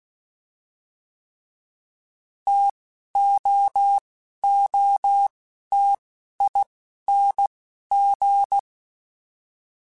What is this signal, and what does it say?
Morse code, heard as a keyed tone, about 16 wpm, spaced as for 11 wpm, 800 Hz. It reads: TOOTING
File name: Difficult Combos / Difficult Combos-mp3s / T AOEI @10